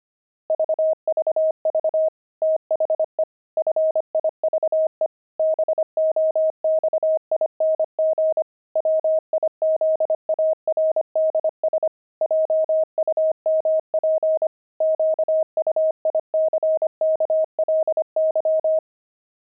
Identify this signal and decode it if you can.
Morse code, heard as a keyed tone, about 25 wpm, 630 Hz. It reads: VVV THE FIVE BOXING WIZARDS JUMP QUICKLY
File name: pangram-fs-25-cs-25-wizards